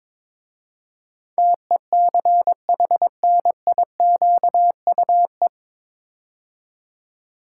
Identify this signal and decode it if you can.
Morse code, heard as a keyed tone, about 22 wpm, 700 Hz. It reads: TECHNIQUE
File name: 9 Letter Words - First 3 Letters Given - 22wpm